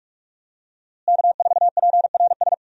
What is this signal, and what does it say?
Morse code, heard as a keyed tone, about 46 wpm, 700 Hz. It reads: K4PRS
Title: USA Call Signs - 5 Characters - Repeated 3x - 45wpm